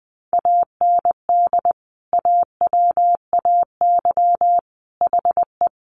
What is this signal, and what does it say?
Morse code, heard as a keyed tone, about 20 wpm, 700 Hz. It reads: AND AWAY HE
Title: Sentences from Top 600 Words 20wpm